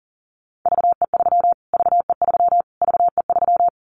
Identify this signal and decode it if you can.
Morse code, heard as a keyed tone, about 37 wpm, 700 Hz. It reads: VE3 VE3 VE3
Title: Callsign Prefixes for Top 25 DXCC - Repeated 3x - 40wpm